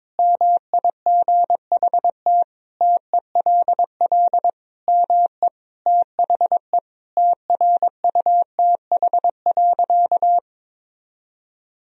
Morse code, heard as a keyed tone, about 22 wpm, 700 Hz. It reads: MIGHT TELL ME THE TRUTH.
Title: Sentences from Top 1000 Words - Review Part 1 - 22wpm